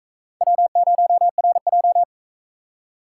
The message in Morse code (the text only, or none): W0RJ